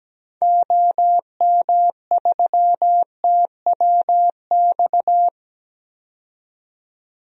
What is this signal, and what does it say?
Morse code, heard as a keyed tone, about 17 wpm, 700 Hz. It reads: OM3TWX
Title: DXCC-100 Call Signs - 6 Characters - 17wpm